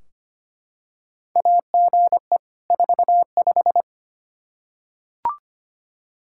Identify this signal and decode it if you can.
Morse code, heard as a keyed tone, about 25 wpm, 700 Hz. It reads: AGE 45
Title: Practice QSO - Exchange 3 - More Info - 25wpm